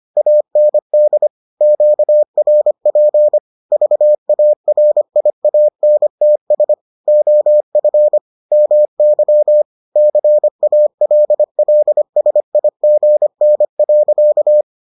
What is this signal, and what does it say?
Morse code, heard as a keyed tone, about 25 wpm, 600 Hz. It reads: AND QRP VARIANTS OF MY CALLSIGN.